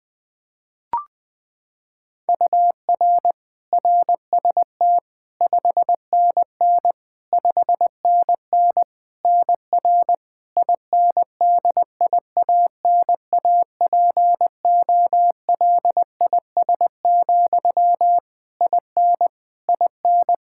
Morse code, heard as a keyed tone, about 20 wpm, 700 Hz. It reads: UR RST 5NN 5NN NR INDIANAPOLIS, IN IN
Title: K1USN SST QSO Practice 20wpm